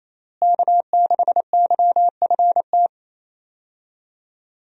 Morse code, heard as a keyed tone, about 28 wpm, 700 Hz. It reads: K6YFT
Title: USA Call Signs - 5 Characters - Repeated 3x - 28wpm